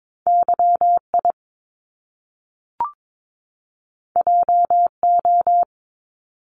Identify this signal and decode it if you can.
Morse code, heard as a keyed tone, about 22 wpm, 700 Hz. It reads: YI JO
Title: Top Two Letter Combinations in Words 22wpm